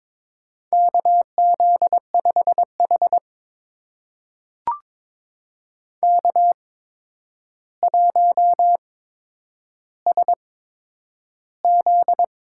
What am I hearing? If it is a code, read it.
Morse code, heard as a keyed tone, about 22 wpm, spaced as for 9 wpm, 700 Hz. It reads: KZ5H K1SZ
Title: USA Call Signs - 4 Characters - 8x Character Spacing - 22wpm